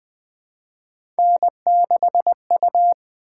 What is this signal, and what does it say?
Morse code, heard as a keyed tone, about 20 wpm, 700 Hz. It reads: N6U
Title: Three Letter-Number Combinations - 20wpm